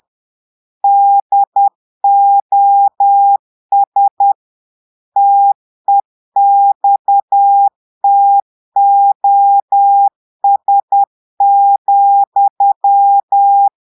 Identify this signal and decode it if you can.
Morse code, heard as a keyed tone, about 10 wpm, 800 Hz. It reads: DOS TEXTOS,
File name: Portugais - texte en clair 1 - vitesse - 10 mots minute